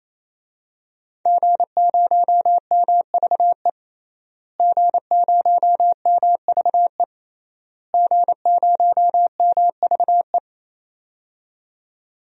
Morse code, heard as a keyed tone, about 28 wpm, 700 Hz. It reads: G0MVE G0MVE G0MVE
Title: DXCC-100 Call Signs - 5 Characters - Repeated 3x - 28wpm